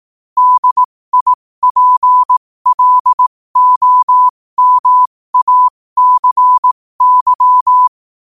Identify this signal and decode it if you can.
Morse code, heard as a keyed tone, about 18 wpm, spaced as for 15 wpm, 1000 Hz. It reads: DIPLOMACY
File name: qotdfiles / 15wpm